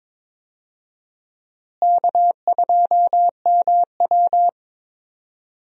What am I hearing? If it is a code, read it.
Morse code, heard as a keyed tone, about 21 wpm, 700 Hz. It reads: K2MW
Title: USA Call Signs - 4 Characters - Repeated 2x - 22wpm